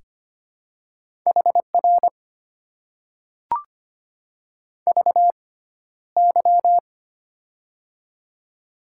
Morse code, heard as a keyed tone, about 25 wpm, spaced as for 13 wpm, 700 Hz. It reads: HR VY
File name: Top 90 QSO Elements - 6x Character Spacing - 25wpm